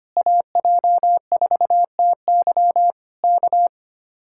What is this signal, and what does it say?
Morse code, heard as a keyed tone, about 24 wpm, 700 Hz. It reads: AJ4TY K
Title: Speed-Racing Common QSO Phrases - 25wpm